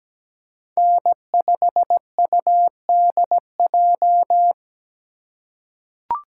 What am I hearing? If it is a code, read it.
Morse code, heard as a keyed tone, about 17 wpm, 700 Hz. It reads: N5UDJ
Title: USA Call Signs - 5 Characters - 2x Character Spacing - 17wpm